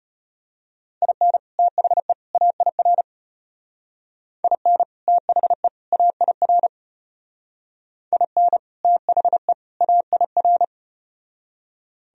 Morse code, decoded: IN THE AIR IN THE AIR IN THE AIR

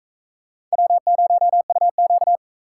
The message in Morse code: W0UQ

